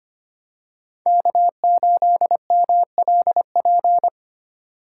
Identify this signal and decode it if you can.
Morse code, heard as a keyed tone, about 25 wpm, 700 Hz. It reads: K8MLP
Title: USA Call Signs - 5 Characters - 2x Character Spacing - 25wpm